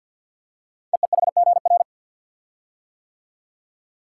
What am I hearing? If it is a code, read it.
Morse code, heard as a keyed tone, about 48 wpm, 700 Hz. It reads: EEHGR
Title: Code Groups - 5 Letters - 50wpm